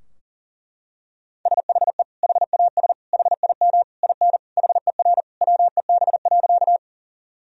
Morse code, heard as a keyed tone, about 40 wpm, 700 Hz. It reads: SHE HAS HIM IN HER WEB.
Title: Sentences from Top 1000 Words - Review Part 4 - 40wpm